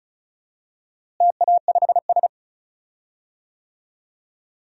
Morse code, heard as a keyed tone, about 36 wpm, 700 Hz. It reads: TA5S
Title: DXCC-100 Call Signs - 4 Characters - Repeated 2x - 35wpm